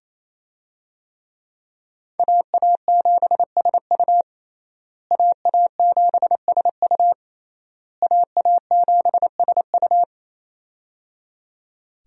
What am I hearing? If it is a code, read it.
Morse code, heard as a keyed tone, about 28 wpm, 700 Hz. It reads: AA7SU AA7SU AA7SU
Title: USA Call Signs - 5 Characters - Repeated 3x - 28wpm